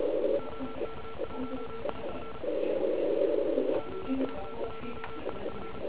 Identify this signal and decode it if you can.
Morse code, heard as a keyed tone, about 36 wpm, 1340 Hz, with noise. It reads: OK1IF OK1IF
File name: Archiv / CUC129 / CUC129 pileup